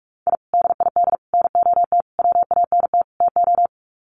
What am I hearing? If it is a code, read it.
Morse code, heard as a keyed tone, about 44 wpm, 700 Hz. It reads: I DID NOT WANT TO
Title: Sets of 5 Words 45wpm